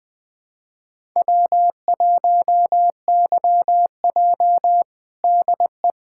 Morse code, heard as a keyed tone, about 20 wpm, 700 Hz. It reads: W1YJ DE
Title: USA Callsign DE Callsign - 4 Characters - 20wpm